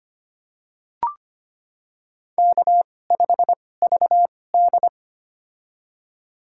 Morse code, heard as a keyed tone, about 25 wpm, spaced as for 18 wpm, 700 Hz. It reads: K5VD
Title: USA Call Signs - 4 Characters - 2x Character Spacing - 25wpm